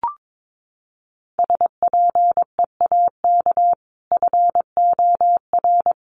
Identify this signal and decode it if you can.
Morse code, heard as a keyed tone, about 22 wpm, 700 Hz. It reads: SPEAK FOR
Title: Sentences from Top 1100 Words 22wpm